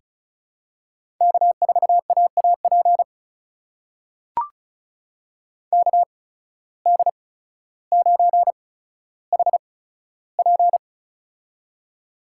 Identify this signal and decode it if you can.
Morse code, heard as a keyed tone, about 36 wpm, spaced as for 15 wpm, 700 Hz. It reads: K4AAP KD9HP
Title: USA Call Signs - 5 Characters - 8x Character Spacing - 35wpm